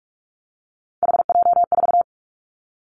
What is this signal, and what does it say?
Morse code, heard as a keyed tone, about 46 wpm, 700 Hz. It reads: HJ4
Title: Callsign Prefixes for Top 100 DXCC - 45wpm